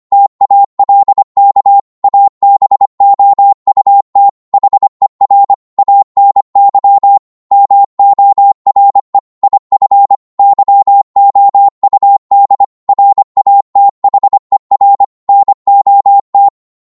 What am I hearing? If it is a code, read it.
Morse code, heard as a keyed tone, about 25 wpm, 800 Hz. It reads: TALK ABOUT HER ANY MORE IF YOUD RATHER NOT